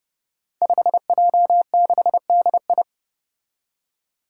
Morse code, heard as a keyed tone, about 29 wpm, 700 Hz. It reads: 5J6DI
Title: Five Challenging Letter-Number Combinations 30wpm